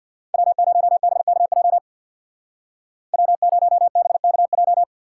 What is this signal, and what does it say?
Morse code, heard as a keyed tone, about 64 wpm, 700 Hz. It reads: W0BKJ W0BKJ